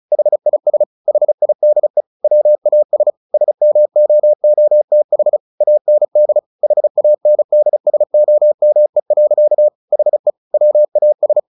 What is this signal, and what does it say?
Morse code, heard as a keyed tone, about 34 wpm, 600 Hz. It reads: HIS HIDE WAS SMOOTH AND HANDSOME. HE WAS
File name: the old man and the sea 2 35wpm